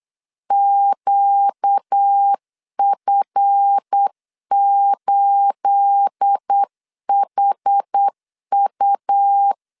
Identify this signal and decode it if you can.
Morse code, heard as a keyed tone, about 8 wpm, 800 Hz. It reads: QF8HU
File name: LEZIONI / 9 28 8wpm 3512